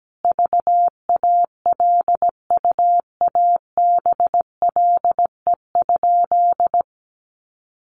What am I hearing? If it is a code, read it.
Morse code, heard as a keyed tone, about 17 wpm, 700 Hz. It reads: VALUABLE?
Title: Sentences from Top 1100 Words 17wpm